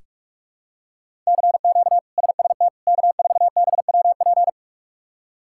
Morse code, heard as a keyed tone, about 46 wpm, 700 Hz. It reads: CQ SST K4BWP
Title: K1USN SST QSO Practice - 45wpm